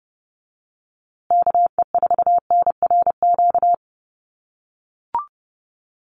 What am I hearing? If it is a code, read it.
Morse code, heard as a keyed tone, about 30 wpm, 700 Hz. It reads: KE4NRQ
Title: USA Call Signs Repeated 2x 30wpm